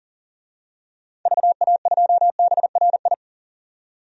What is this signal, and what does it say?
Morse code, heard as a keyed tone, about 40 wpm, 700 Hz. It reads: VA2BRI
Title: DXCC-100 Call Signs - 6 Characters - 40wpm